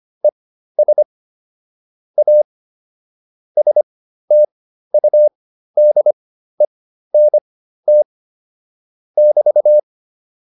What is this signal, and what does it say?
Morse code, heard as a keyed tone, about 25 wpm, spaced as for 13 wpm, 600 Hz. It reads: ES A STUDENT <BT>
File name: QSO303 13